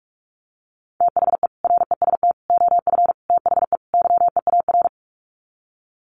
Morse code, heard as a keyed tone, about 44 wpm, 700 Hz. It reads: THE REST OF THE YEAR
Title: Sets of 5 Words Encore 45wpm